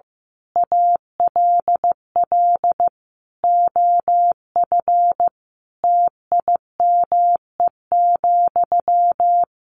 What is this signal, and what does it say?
Morse code, heard as a keyed tone, about 15 wpm, 700 Hz. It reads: ALL OF TIME,
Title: A Sound Of Thunder 15wpm